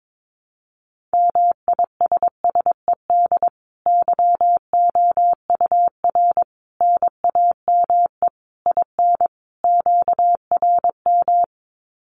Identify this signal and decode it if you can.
Morse code, heard as a keyed tone, about 22 wpm, 700 Hz. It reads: MISSED YOUR NAME IN QRM
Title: Common QSO Phrases 22wpm